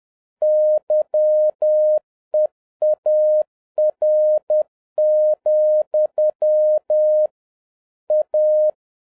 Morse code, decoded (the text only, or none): YEAR, A